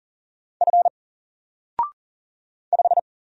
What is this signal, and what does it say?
Morse code, heard as a keyed tone, about 38 wpm, 700 Hz. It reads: F 5
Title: Single Letter-Number Alt 40wpm